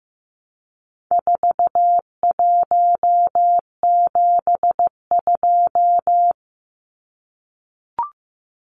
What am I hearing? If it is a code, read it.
Morse code, heard as a keyed tone, about 14 wpm, 700 Hz. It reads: 4172
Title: Numbers - 4-Digit 15wpm